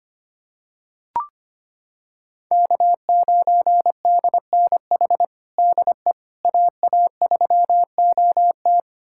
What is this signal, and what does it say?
Morse code, heard as a keyed tone, about 25 wpm, 700 Hz. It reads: K9DNH DE AA3OT
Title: USA Callsign DE Callsign - 5 Characters - 25wpm